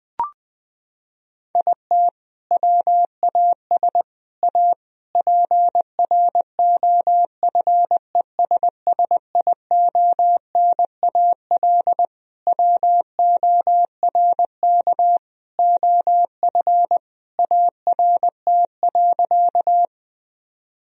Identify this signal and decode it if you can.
Morse code, heard as a keyed tone, about 21 wpm, 700 Hz. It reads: IT WAS A PROFESSIONAL WORK OF ART.
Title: Sentences from Top 1000 Words - Review Part 4 - 20wpm